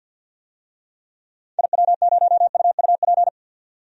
Morse code, heard as a keyed tone, about 50 wpm, 700 Hz. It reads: IW0UUP
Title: DXCC-100 Call Signs - 6 Characters - 50wpm